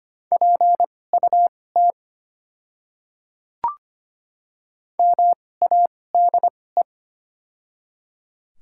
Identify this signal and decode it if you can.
Morse code, heard as a keyed tone, about 25 wpm, spaced as for 18 wpm, 700 Hz. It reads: PUT MADE
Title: 2x Character Spacing - Top 200 Words - 25wpm